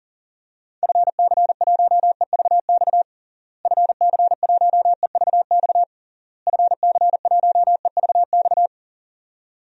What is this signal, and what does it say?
Morse code, heard as a keyed tone, about 40 wpm, 700 Hz. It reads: FC1EVX FC1EVX FC1EVX
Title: DXCC-100 Call Signs - 6 Characters - Repeated 3x - 40wpm